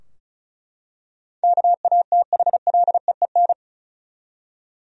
Morse code, heard as a keyed tone, about 36 wpm, 700 Hz. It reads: KATHLEEN
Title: USA Common Names - 35wpm